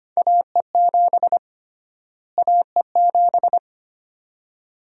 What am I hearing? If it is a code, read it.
Morse code, heard as a keyed tone, about 25 wpm, 700 Hz. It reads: AE7 AE7
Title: USA Call Sign Prefixes - Repeated 2x - 25wpm